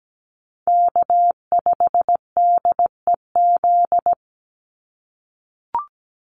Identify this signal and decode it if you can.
Morse code, heard as a keyed tone, about 17 wpm, 700 Hz. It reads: K5DEZ
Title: NA SOTA Chaser Callsigns 17wpm